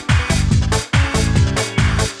There is no code.